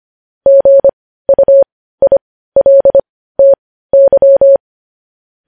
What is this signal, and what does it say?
Morse code, heard as a keyed tone, about 25 wpm, spaced as for 15 wpm, 550 Hz. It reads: GUILTY
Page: news-headlines-in-morse-code-at-15-wpm